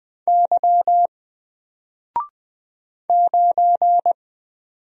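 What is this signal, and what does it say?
Morse code, heard as a keyed tone, about 20 wpm, 700 Hz. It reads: Y 9
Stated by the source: Single letter and number Morse drill at 20 wpm, from N4LCD.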